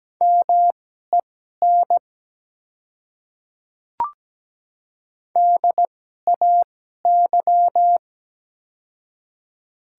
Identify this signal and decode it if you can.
Morse code, heard as a keyed tone, about 17 wpm, spaced as for 12 wpm, 700 Hz. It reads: MEN DAY
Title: 2x Character Spacing - Top 200 Words - 17wpm